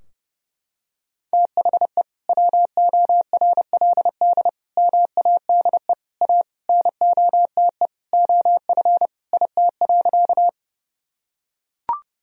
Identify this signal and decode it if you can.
Morse code, heard as a keyed tone, about 32 wpm, 700 Hz. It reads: THE WORLD MADE A NOTE OF IT.
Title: Sentences from Top 1000 Words - Review Part 2 - 30wpm